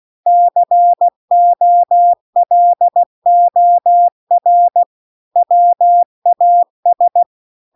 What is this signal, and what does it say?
Morse code, heard as a keyed tone, about 16 wpm, 700 Hz. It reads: COLOR WAS
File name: CW-The-Big-Twoheaded-River-2-16WPM